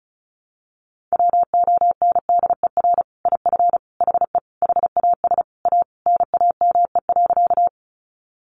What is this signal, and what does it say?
Morse code, heard as a keyed tone, about 34 wpm, 700 Hz. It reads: WONDER IF HE HAS A NAME.
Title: Sentences from Top 500 Words 35wpm